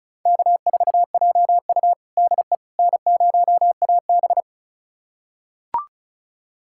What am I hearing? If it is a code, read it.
Morse code, heard as a keyed tone, about 34 wpm, 700 Hz. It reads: K4JU DE N0AB
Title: USA Callsign DE Callsign - 4 Characters - 35wpm